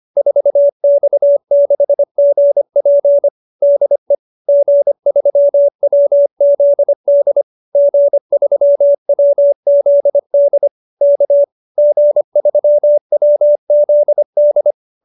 Morse code, heard as a keyed tone, about 25 wpm, 590 Hz. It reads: 4X6GP DE G3WZD G3WZD K G3WZD
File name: QSO310 25